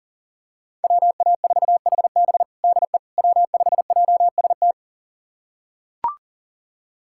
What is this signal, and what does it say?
Morse code, heard as a keyed tone, about 40 wpm, 700 Hz. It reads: WA4HB DE W5JST
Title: USA Callsign DE Callsign - 5 Characters - 40wpm